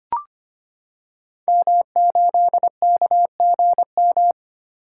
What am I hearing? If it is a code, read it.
Morse code, heard as a keyed tone, about 25 wpm, 700 Hz. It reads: M8KGM
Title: Code Groups - 5 Letter:Numbers - 25wpm